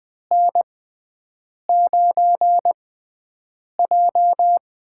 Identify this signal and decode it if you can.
Morse code, heard as a keyed tone, about 20 wpm, spaced as for 7 wpm, 700 Hz. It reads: N9J
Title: USA Call Signs - 4 Characters - 6x Character Spacing - 20wpm